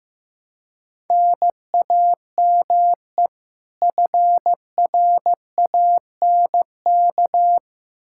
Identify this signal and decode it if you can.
Morse code, heard as a keyed tone, about 15 wpm, 700 Hz. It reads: NAME FRANK
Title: Practice QSO - Exchange 1 - Beginning - 15wpm